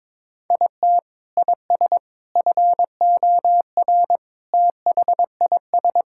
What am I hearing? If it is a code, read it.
Morse code, heard as a keyed tone, about 22 wpm, 700 Hz. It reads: IT IS FOR THIS
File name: practice / 22wpm / Sets of 6 Words - 22wpm